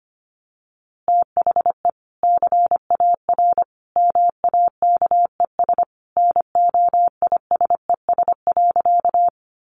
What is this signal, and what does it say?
Morse code, heard as a keyed tone, about 25 wpm, 700 Hz. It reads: THE CAR MAKES NOISES.
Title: Easy Phrases 25wpm